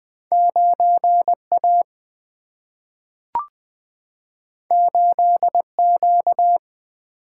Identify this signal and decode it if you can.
Morse code, heard as a keyed tone, about 20 wpm, 700 Hz. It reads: 9 A 8 Q
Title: Single-Letter Single-Number 20wpm